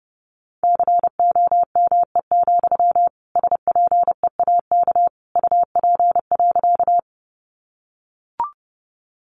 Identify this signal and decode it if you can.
Morse code, heard as a keyed tone, about 30 wpm, 700 Hz. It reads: COME, SPEAK UP.
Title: Sentences from Top 1100 Words 30wpm